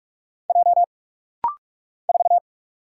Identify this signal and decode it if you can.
Morse code, heard as a keyed tone, about 45 wpm, 700 Hz. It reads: J 4
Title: Single Letter-Number Alt - Rapid-Fire 45wpm